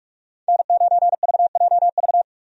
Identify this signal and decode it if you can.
Morse code, heard as a keyed tone, about 43 wpm, 700 Hz. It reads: N9VJV
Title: USA Call Signs - 5 Characters - 45wpm